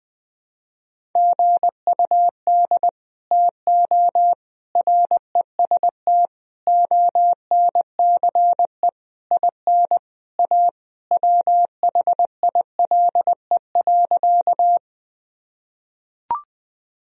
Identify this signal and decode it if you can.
Morse code, heard as a keyed tone, about 20 wpm, 700 Hz. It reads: GUD TO REST ONCE IN A WHILE.
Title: Sentences from Top 500 Words and Abbreviations 20wpm